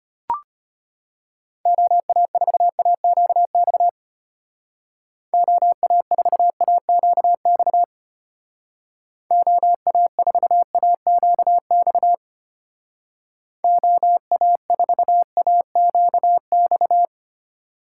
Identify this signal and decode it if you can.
Morse code, heard as a keyed tone, about 32 wpm, 700 Hz. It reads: OA4AQX OA4AQX OA4AQX OA4AQX